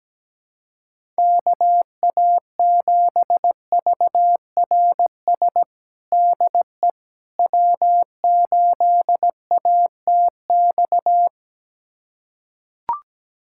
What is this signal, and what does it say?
Morse code, heard as a keyed tone, about 17 wpm, 700 Hz. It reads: KA7VRS DE W8ATX
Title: USA Callsign DE Callsign 17wpm